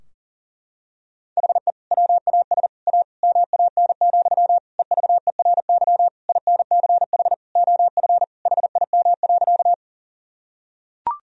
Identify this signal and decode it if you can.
Morse code, heard as a keyed tone, about 40 wpm, 700 Hz. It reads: HE WAS A MAN, EVERY INCH OF HIM.
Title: Sentences from Top 1000 Words - Review Part 3 - 40wpm